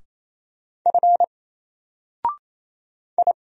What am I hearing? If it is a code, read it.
Morse code, heard as a keyed tone, about 27 wpm, 700 Hz. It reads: F I
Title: Single Character (Letter:Number:Punctuation:Prosign) Alt 28wpm